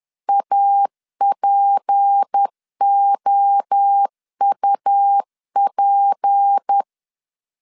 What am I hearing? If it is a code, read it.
Morse code, heard as a keyed tone, about 10 wpm, 800 Hz. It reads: APOUP
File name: LEZIONI / 6 PKX0 / 6 P 10wpm 2432